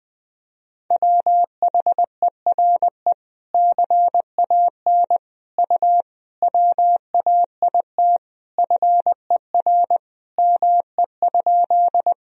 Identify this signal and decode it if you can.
Morse code, heard as a keyed tone, about 20 wpm, 700 Hz. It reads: WHERE CAN U WAIT FER ME?
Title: Sentences from Top 500 Words and Abbreviations 20wpm